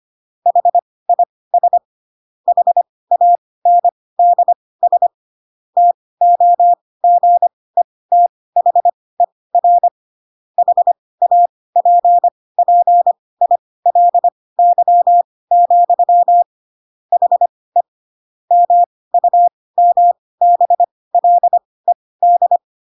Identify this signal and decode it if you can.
Morse code, heard as a keyed tone, about 25 wpm, spaced as for 18 wpm, 700 Hz. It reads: HIS HANDS TOGETHER HAPPILY, HE MUMBLED
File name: Pinocchio / Pinocchio ch1 12 25 ik1hge